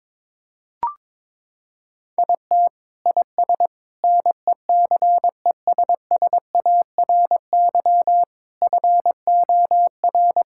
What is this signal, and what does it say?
Morse code, heard as a keyed tone, about 21 wpm, 700 Hz. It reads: IT IS NECESSARY FOR
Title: Sentences from Top 1000 Words - Review Part 5 - 22wpm